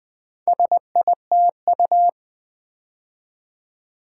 Morse code, heard as a keyed tone, about 20 wpm, 700 Hz. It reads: SITU
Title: Build A Word - 4 Letters At A Time - 8-Letter Words - 20wpm